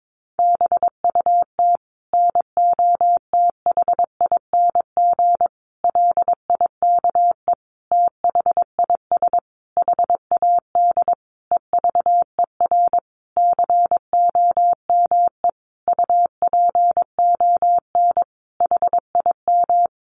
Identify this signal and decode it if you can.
Morse code, heard as a keyed tone, about 22 wpm, 700 Hz. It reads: BUT NOTHING LIKE THIS HAD EVER COME UPON HIM
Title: Sentences from Top 1100 Words 22wpm